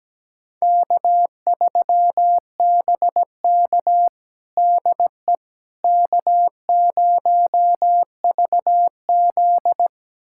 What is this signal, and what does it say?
Morse code, heard as a keyed tone, about 17 wpm, 700 Hz. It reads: K3BK DE K0VZ
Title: USA Callsign DE Callsign - 4 Characters - 17wpm